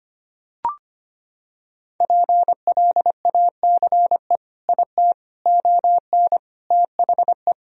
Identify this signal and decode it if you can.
Morse code, heard as a keyed tone, about 25 wpm, 700 Hz. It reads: PLACE IT ON THE
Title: Sentences from Top 1000 Words - Review Part 2 - 25wpm